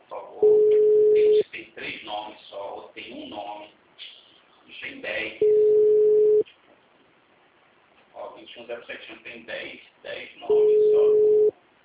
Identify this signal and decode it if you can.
Ringback tone (Europe, 425 Hz), with speech.